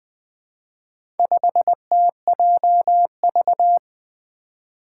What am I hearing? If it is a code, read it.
Morse code, heard as a keyed tone, about 20 wpm, 700 Hz. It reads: 5TJV
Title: Four Letter-Number Combinations - 20wpm